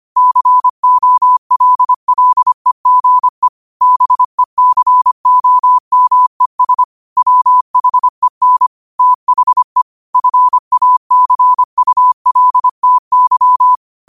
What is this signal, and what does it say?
Morse code, heard as a keyed tone, about 25 wpm, 1000 Hz. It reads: COLLEGE BECOMES WHEN THE FACULTY
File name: qotdfiles / 25wpm